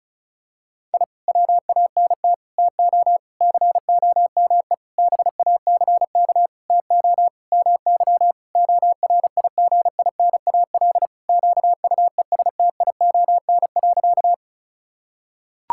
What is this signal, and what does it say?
Morse code, heard as a keyed tone, about 36 wpm, 700 Hz. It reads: I WANT TO COME BACK TO MY ORIGINAL QUESTION.
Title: Sentences from Top 1000 Words - Review Part 6 - 35wpm